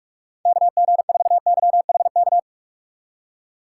Morse code, heard as a keyed tone, about 43 wpm, 700 Hz. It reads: KG4YHK
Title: USA Call Signs - 6 Characters - 3x Character Spacing - 45wpm